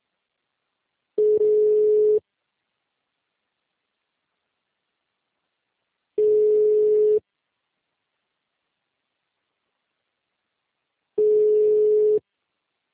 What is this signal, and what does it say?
Ringback tone (Europe, 425 Hz).